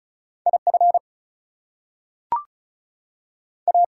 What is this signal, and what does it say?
Morse code, heard as a keyed tone, about 36 wpm, 700 Hz. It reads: IF A
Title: Two Letter Words - 35wpm